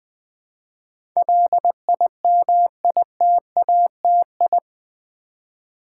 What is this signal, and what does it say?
Morse code, heard as a keyed tone, about 20 wpm, 700 Hz. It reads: LIMITATI
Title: Build A Word - 4 Letters At A Time - 10-Letter Words - 20wpm